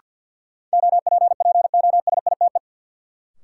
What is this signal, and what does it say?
Morse code, heard as a keyed tone, about 51 wpm, 700 Hz. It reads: OPPOSITE
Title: Sets of 5 Words Encore - New Word Drill - 50wpm